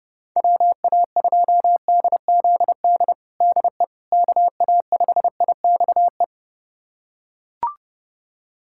Morse code, heard as a keyed tone, about 30 wpm, 700 Hz. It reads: WA2DZD DE KA5IXE
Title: USA Callsign DE Callsign - 6 Characters - 30wpm